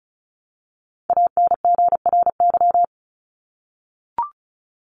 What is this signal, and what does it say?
Morse code, heard as a keyed tone, about 34 wpm, 700 Hz. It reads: ANGRY
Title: Top 1751-2000 Words 35wpm